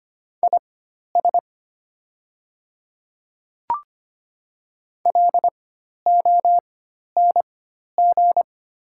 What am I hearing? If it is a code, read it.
Morse code, heard as a keyed tone, about 25 wpm, spaced as for 12 wpm, 700 Hz. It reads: IS LONG
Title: 4x Character Spacing - Top 500 Words - 25wpm